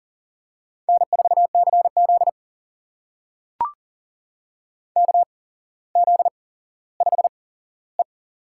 Morse code, heard as a keyed tone, about 40 wpm, spaced as for 17 wpm, 700 Hz. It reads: N4CZ KZ5E
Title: USA Call Signs - 8x Character Spacing - 40wpm